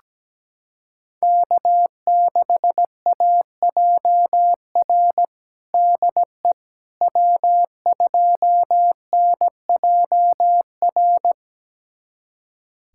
Morse code, heard as a keyed tone, about 17 wpm, 700 Hz. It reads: K6AJR DE W2NJR
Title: USA Callsign DE Callsign - 5 Characters - 17wpm